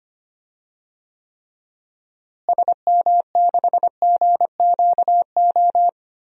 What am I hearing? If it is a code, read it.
Morse code, heard as a keyed tone, about 25 wpm, 700 Hz. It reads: SM6GQO
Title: DXCC-100 Call Signs - 6 Characters - Repeated 2x - 25wpm